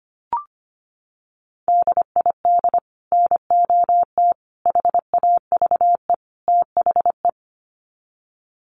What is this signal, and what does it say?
Morse code, heard as a keyed tone, about 25 wpm, 700 Hz. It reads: DID NOT HAVE THE
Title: Sets of 4 Words 25wpm